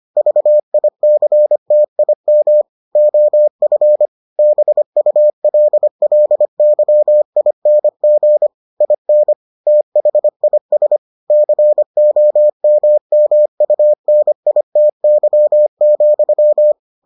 Morse code, heard as a keyed tone, about 25 wpm, 600 Hz. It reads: VICTIM OF BULLYING IN THIS COMMUNITY,